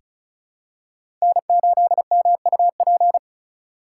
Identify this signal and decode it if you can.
Morse code, heard as a keyed tone, about 34 wpm, 700 Hz. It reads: N8MUP
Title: USA Call Signs - 5 Characters - 4x Character Spacing - 35wpm